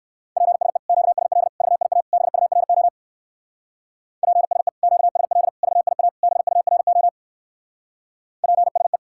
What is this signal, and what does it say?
Morse code, heard as a keyed tone, about 57 wpm, 700 Hz. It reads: PSE QSL VIA BURO PSE QSL VIA BURO PSE